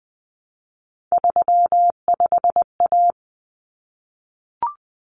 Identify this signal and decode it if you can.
Morse code, heard as a keyed tone, about 20 wpm, 700 Hz. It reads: 35A
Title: Common Cut Numbers - 3-Digit 20wpm